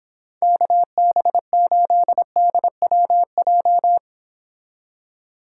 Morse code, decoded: KB8DWJ

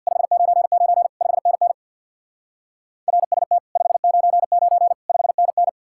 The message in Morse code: 599 5NN RST 599 5NN

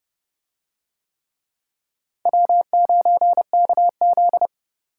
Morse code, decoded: W9KZ

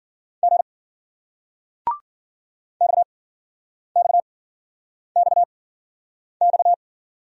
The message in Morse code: G XXXX

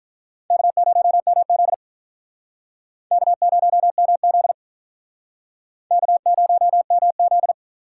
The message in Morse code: K0MZ K0MZ K0MZ

